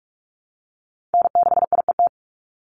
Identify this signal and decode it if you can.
Morse code, heard as a keyed tone, about 47 wpm, 700 Hz. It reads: N6IET
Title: Call Signs Repeated 2x - 45wpm